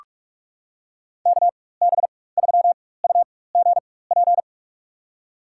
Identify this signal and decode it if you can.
Morse code, heard as a keyed tone, about 41 wpm, spaced as for 20 wpm, 700 Hz. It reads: KB3UGP
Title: USA Call Signs - 6 Characters - 4x Character Spacing - 45wpm